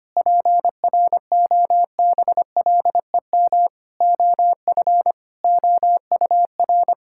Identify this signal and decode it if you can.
Morse code, heard as a keyed tone, about 25 wpm, 700 Hz. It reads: PROBLEM OF OUR
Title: Sentences from Top 1000 Words - Review Part 5 - 25wpm